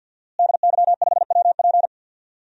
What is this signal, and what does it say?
Morse code, heard as a keyed tone, about 51 wpm, 700 Hz. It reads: DY5WP
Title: DXCC-100 Call Signs - 5 Characters - Repeated 3x - 50wpm